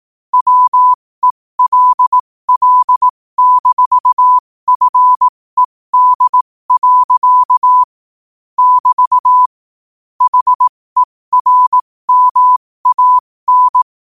Morse code, heard as a keyed tone, about 18 wpm, spaced as for 15 wpm, 1000 Hz. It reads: WELL-FED. <BT> HERMAN
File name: qotdfiles / 15wpm